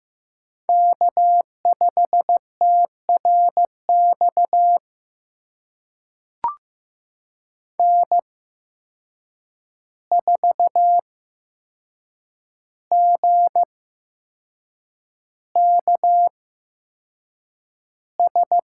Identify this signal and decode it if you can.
Morse code, heard as a keyed tone, about 15 wpm, spaced as for 6 wpm, 700 Hz. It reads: K5TRX N4GKS
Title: USA Call Signs - 5 Characters - 8x Character Spacing - 15wpm